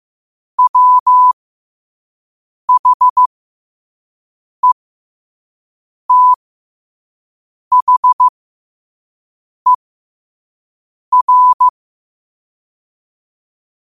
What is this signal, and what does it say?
Morse code, heard as a keyed tone, about 15 wpm, spaced as for 5 wpm, 1000 Hz. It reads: WHETHER
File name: qotdfiles / 5wpm